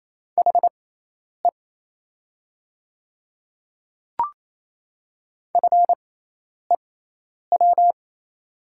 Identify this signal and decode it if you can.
Morse code, heard as a keyed tone, about 27 wpm, spaced as for 10 wpm, 690 Hz. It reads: HE FEW
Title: 6x Character Spacing - Top 300 Words - 28wpm